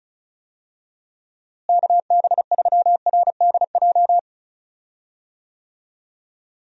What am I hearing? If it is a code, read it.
Morse code, heard as a keyed tone, about 34 wpm, 700 Hz. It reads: KB3RDJ
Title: USA Call Signs - 6 Characters - Repeated 2x - 35wpm